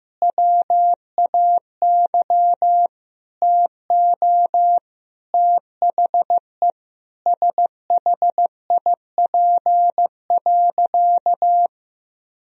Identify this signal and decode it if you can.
Morse code, heard as a keyed tone, about 16 wpm, 700 Hz. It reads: WAY TO THE SHIP.
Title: Sentences from Top 1000 Words - Review Part 2 - 15wpm